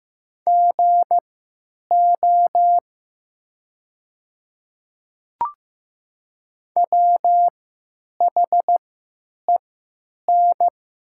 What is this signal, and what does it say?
Morse code, heard as a keyed tone, about 15 wpm, spaced as for 9 wpm, 700 Hz. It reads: GO WHEN
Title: 3x Character Spacing - Top 100 Words - 15wpm